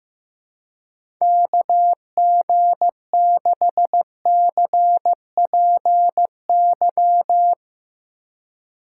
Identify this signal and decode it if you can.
Morse code, heard as a keyed tone, about 15 wpm, 700 Hz. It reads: KG6CPY
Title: USA Call Signs - 2x Character Spacing - 15wpm